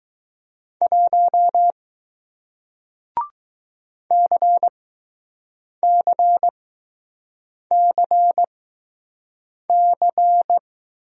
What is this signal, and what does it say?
Morse code, heard as a keyed tone, about 20 wpm, spaced as for 6 wpm, 700 Hz. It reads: 1 CCCC